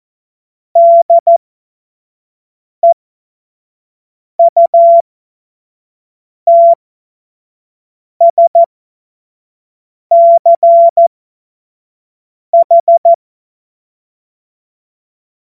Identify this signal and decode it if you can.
Morse code, heard as a keyed tone, about 14 wpm, spaced as for 5 wpm, 680 Hz. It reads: DEUTSCH
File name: Witz-14-Oktober-2025-25-F70